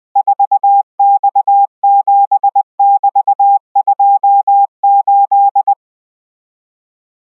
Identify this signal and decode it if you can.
Morse code, heard as a keyed tone, about 20 wpm, 800 Hz. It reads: 4X7<BT>28